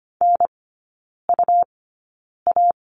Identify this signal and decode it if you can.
Morse code, heard as a keyed tone, about 25 wpm, spaced as for 9 wpm, 700 Hz. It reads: NUA
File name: Lesson 17e - Letter u - Three Characters 25wpm